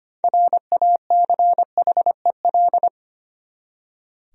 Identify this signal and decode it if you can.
Morse code, heard as a keyed tone, about 25 wpm, 700 Hz. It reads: RACHEL
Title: USA Common Names - 25wpm